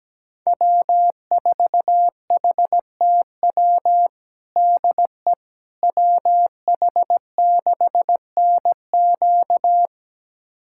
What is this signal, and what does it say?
Morse code, heard as a keyed tone, about 17 wpm, 700 Hz. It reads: W4HTW DE WH6NQ
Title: USA Callsign DE Callsign - 5 Characters - 17wpm